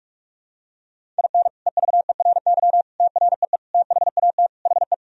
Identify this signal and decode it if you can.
Morse code, heard as a keyed tone, about 46 wpm, 700 Hz. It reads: IN EVERY TREE THAT HE
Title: Sentences from Top 1000 Words - Review Part 1 - 45wpm